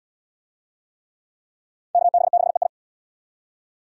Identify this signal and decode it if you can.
Morse code, heard as a keyed tone, about 66 wpm, 700 Hz. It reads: KB6EI